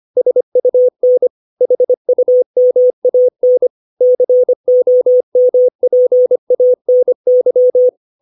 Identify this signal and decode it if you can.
Morse code, heard as a keyed tone, about 25 wpm, 500 Hz. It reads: SUN HUMAN COMPANY
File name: 632 words / 632 Words 25WPM